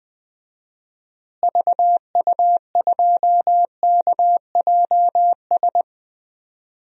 Morse code, heard as a keyed tone, about 20 wpm, 700 Hz. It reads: VU2KJS
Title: DXCC-100 Call Signs - 6 Characters - Repeated 3x - 20wpm